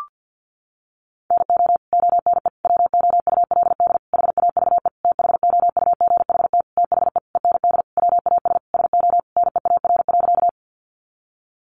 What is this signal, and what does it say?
Morse code, heard as a keyed tone, about 49 wpm, 700 Hz. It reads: NO ONE WOULD HAVE THOUGHT THE END WAS SO NEAR.
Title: Sentences from Top 300 Words 50wpm